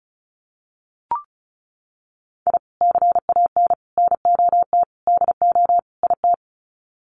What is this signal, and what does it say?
Morse code, heard as a keyed tone, about 34 wpm, 700 Hz. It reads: I CAN NOT DO IT
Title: Sets of 5 Words Encore 35wpm